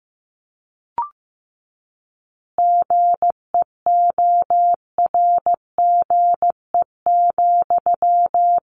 Morse code, heard as a keyed tone, about 15 wpm, 700 Hz. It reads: GEORGE,
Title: Additional Easy Phrases 15wpm